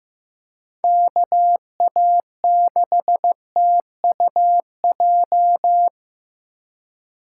Morse code, heard as a keyed tone, about 15 wpm, 700 Hz. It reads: KA6TUJ
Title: USA Call Signs - 6 Characters - 4x Character Spacing - 15wpm